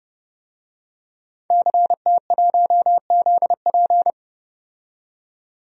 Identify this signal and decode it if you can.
Morse code, heard as a keyed tone, about 30 wpm, 700 Hz. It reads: CT1ZP
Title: DXCC-100 Call Signs - 5 Characters - 30wpm